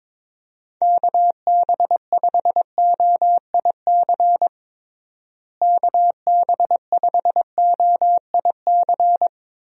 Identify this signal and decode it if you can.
Morse code, heard as a keyed tone, about 22 wpm, 700 Hz. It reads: KB5OIC KB5OIC
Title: USA Call Signs - 6 Characters - Repeated 2x - 22wpm